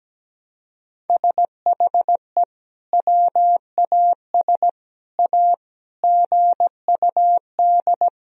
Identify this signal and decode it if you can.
Morse code, heard as a keyed tone, about 17 wpm, 700 Hz. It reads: SHE WAS A GUD
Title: Sentences from Top 400 Words and Abbreviations 17wpm